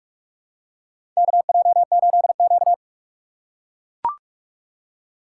Morse code, heard as a keyed tone, about 44 wpm, 690 Hz. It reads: KJ8Q
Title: USA Call Signs - 4 Characters - 4x Character Spacing - 45wpm